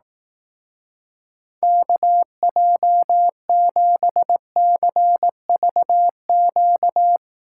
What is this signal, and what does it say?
Morse code, heard as a keyed tone, about 20 wpm, 700 Hz. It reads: KJ7CVQ